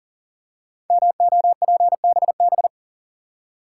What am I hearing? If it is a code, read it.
Morse code, heard as a keyed tone, about 39 wpm, 700 Hz. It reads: MOPBB
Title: Code Groups - 5 Letters - 40wpm